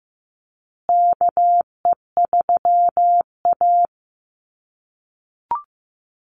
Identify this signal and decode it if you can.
Morse code, heard as a keyed tone, about 15 wpm, 700 Hz. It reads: KE3A
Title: USA Call Signs Repeated 2x 15wpm